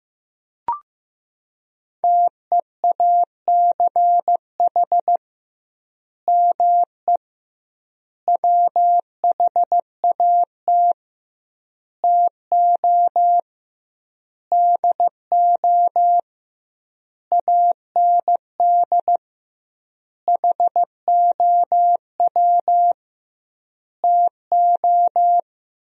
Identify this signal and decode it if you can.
Morse code, heard as a keyed tone, about 15 wpm, 700 Hz. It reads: TEACH ME WHAT TO DO AND HOW TO
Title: 2x Word-Spacing - Sentences from Top 500 Words - 15wpm